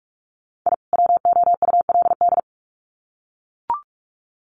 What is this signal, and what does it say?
Morse code, heard as a keyed tone, about 44 wpm, 700 Hz. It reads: I WOULD
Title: Sets of 2 Words 45wpm